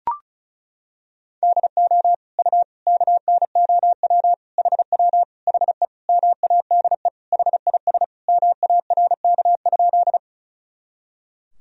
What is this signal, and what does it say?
Morse code, heard as a keyed tone, about 36 wpm, 700 Hz. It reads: DO U KNOW HW HE MADE HIS MARK?
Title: Sentences from Top 300 Words and Abbreviations 35wpm